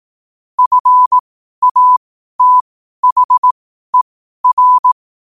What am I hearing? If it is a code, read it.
Morse code, heard as a keyed tone, about 18 wpm, spaced as for 12 wpm, 1000 Hz. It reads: FATHER
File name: qotdfiles / 12wpm